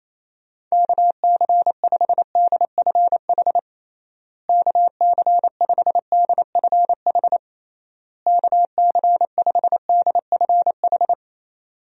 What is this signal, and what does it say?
Morse code, heard as a keyed tone, about 28 wpm, 700 Hz. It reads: KC5DFH KC5DFH KC5DFH
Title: USA Call Signs - 6 Characters - Repeated 3x - 28wpm